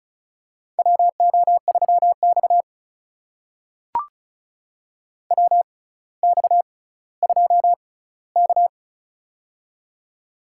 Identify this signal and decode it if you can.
Morse code, heard as a keyed tone, about 36 wpm, spaced as for 18 wpm, 700 Hz. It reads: WO3X WX2K
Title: USA Call Signs - 4 Characters - 6x Character Spacing - 35wpm